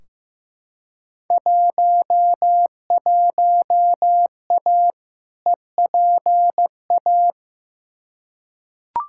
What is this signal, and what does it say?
Morse code, heard as a keyed tone, about 15 wpm, 700 Hz. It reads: 11A EPA
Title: Field Day Exchanges Encore - 15wpm